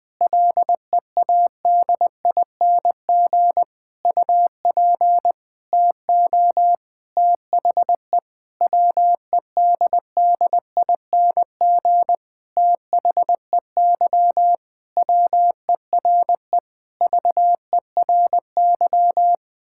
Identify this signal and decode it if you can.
Morse code, heard as a keyed tone, about 20 wpm, 700 Hz. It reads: LEADING UP TO THE WEDDING THEY WERE VERY
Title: Sentences with 6-Word Phrases - 20wpm